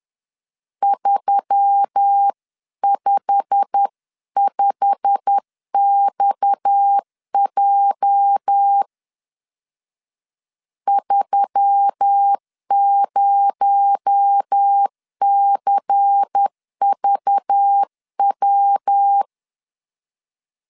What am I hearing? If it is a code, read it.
Morse code, heard as a keyed tone, about 10 wpm, 800 Hz. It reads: 355XJ 30CVW